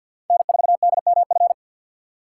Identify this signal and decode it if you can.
Morse code, heard as a keyed tone, about 47 wpm, 700 Hz. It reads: N4DMF
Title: USA Call Signs - 5 Characters - 4x Character Spacing - 50wpm